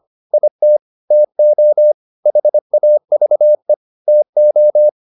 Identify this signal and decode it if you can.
Morse code, heard as a keyed tone, about 25 wpm, 600 Hz. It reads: IT TO HAVE TO